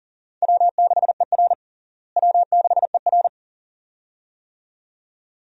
Morse code, heard as a keyed tone, about 40 wpm, 700 Hz. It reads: W6ER W6ER
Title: USA Call Signs - 4 Characters - Repeated 2x - 40wpm